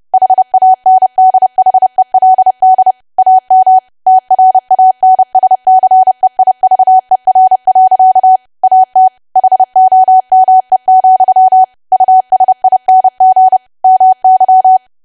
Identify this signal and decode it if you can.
Morse code, heard as a keyed tone, about 30 wpm, 750 Hz. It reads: HANDHELD AM TRANSCEIVER. AT HOME, USING MY
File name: cwa / W1AW-Level3 / 30WPM